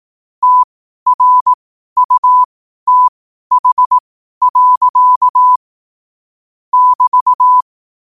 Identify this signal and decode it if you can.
Morse code, heard as a keyed tone, about 18 wpm, spaced as for 12 wpm, 1000 Hz. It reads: TRUTH. <BT>
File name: qotdfiles / 12wpm